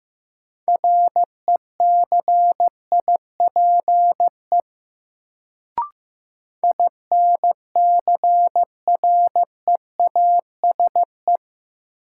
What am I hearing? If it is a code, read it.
Morse code, heard as a keyed tone, about 15 wpm, 700 Hz. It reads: RECIPE INCREASE
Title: Sentences from Top 1200 Words - New Word Drill Alt - 15wpm